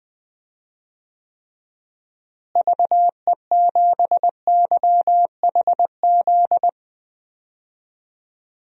Morse code, heard as a keyed tone, about 20 wpm, 700 Hz. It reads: VE7YHZ
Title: DXCC-100 Call Signs - 6 Characters - Repeated 3x - 20wpm